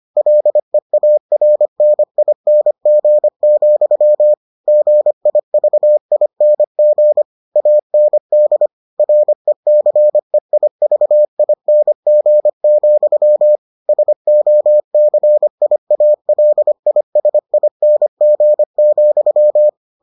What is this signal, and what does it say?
Morse code, heard as a keyed tone, about 25 wpm, 600 Hz. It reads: LEARNING, GIVING AND RECEIVING, SOCIALISING,